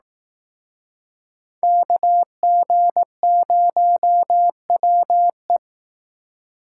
Morse code, heard as a keyed tone, about 18 wpm, 700 Hz. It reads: KG0WE
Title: Speed Racing - USA 2x2 Call Signs 15wpm